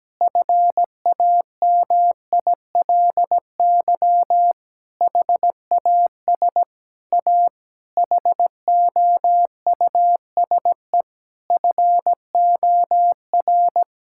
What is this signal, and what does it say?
Morse code, heard as a keyed tone, about 17 wpm, 700 Hz. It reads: FAMILY HAS A HOUSE FOR
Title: Sentences from Top 1000 Words - Review Part 5 - 17wpm